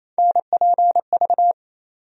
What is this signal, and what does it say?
Morse code, heard as a keyed tone, about 28 wpm, 700 Hz. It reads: NPV
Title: USA Call Sign Postfixes - 3 Letters 28wpm